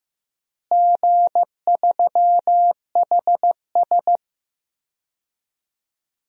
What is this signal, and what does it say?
Morse code, heard as a keyed tone, about 16 wpm, 700 Hz. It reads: G3HS
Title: DXCC-100 Call Signs - 4 Characters - Repeated 3x - 15wpm